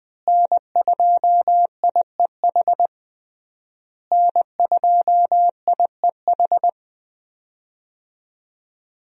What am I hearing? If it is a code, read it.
Morse code, heard as a keyed tone, about 20 wpm, 700 Hz. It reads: N2IEH N2IEH
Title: USA Call Signs - 5 Characters - Repeated 3x - 20wpm